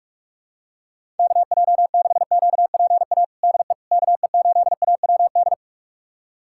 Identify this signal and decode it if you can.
Morse code, heard as a keyed tone, about 41 wpm, 700 Hz. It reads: KJ6QPA DE KE8AWD
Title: USA Callsign DE Callsign - 6 Characters - 45wpm